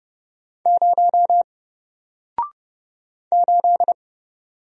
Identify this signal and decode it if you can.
Morse code, heard as a keyed tone, about 31 wpm, 700 Hz. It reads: 0 8
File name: Single-Letter-Number-30wpm